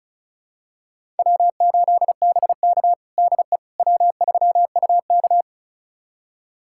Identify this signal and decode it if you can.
Morse code, heard as a keyed tone, about 36 wpm, 700 Hz. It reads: W8BK DE W3UK
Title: USA Callsign DE Callsign - 4 Characters - 35wpm